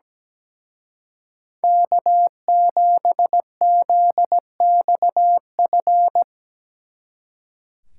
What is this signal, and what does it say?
Morse code, heard as a keyed tone, about 19 wpm, 700 Hz. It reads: K7ZXF
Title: Speed Racing - USA Call Signs 17wpm